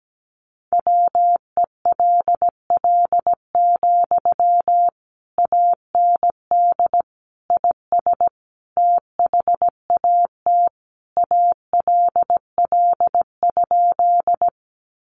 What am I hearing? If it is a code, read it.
Morse code, heard as a keyed tone, about 17 wpm, 700 Hz. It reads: WELL, AND IS THAT ALL?
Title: Sentences from Top 200 Words 17wpm